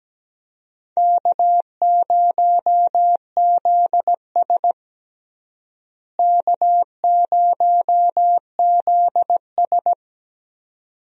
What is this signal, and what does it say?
Morse code, heard as a keyed tone, about 17 wpm, 700 Hz. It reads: K0ZS K0ZS
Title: USA Call Signs - 4 Characters - Repeated 2x - 17wpm